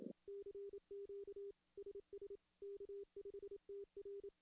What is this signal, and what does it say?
Morse code, heard as a keyed tone, about 25 wpm, 400 Hz. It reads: CQ SS K5TR
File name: contests / CW-SS-k5tr- -1600